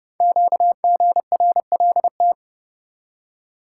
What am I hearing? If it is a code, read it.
Morse code, heard as a keyed tone, about 28 wpm, 700 Hz. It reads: QGRLT
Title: Code Groups - 5 Letters - 30wpm